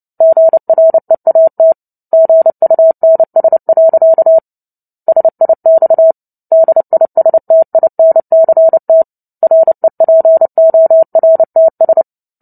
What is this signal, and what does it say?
Morse code, heard as a keyed tone, about 29 wpm, 650 Hz. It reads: GREAT GUNS. SIX DISTINCT REPORTS
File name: Book Two / 03-Chapter 3 - 29 WPM